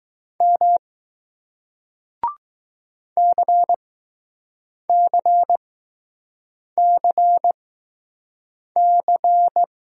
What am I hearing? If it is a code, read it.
Morse code, heard as a keyed tone, about 20 wpm, spaced as for 6 wpm, 700 Hz. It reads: M CCCC